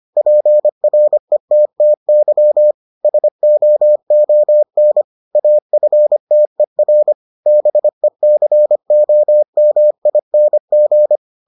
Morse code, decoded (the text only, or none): PRETTY SOON AFTER BECOMING